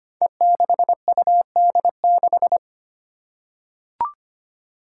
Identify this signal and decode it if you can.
Morse code, heard as a keyed tone, about 25 wpm, 700 Hz. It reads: E6UD6
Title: Five Challenging Letter-Number Combinations 25wpm